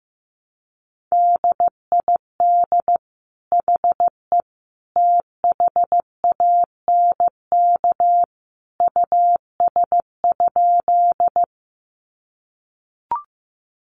Morse code, heard as a keyed tone, about 15 wpm, 700 Hz. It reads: DID HE THANK US?
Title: Sentences from Top 600 Words 15wpm